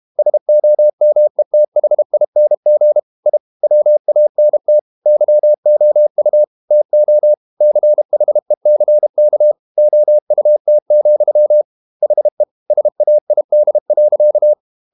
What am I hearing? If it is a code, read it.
Morse code, heard as a keyed tone, about 32 wpm, 600 Hz. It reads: SOMETHING I WANT YOU TO CHECK OUT, HE SAID.
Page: Never 31wpm | CW med Gnister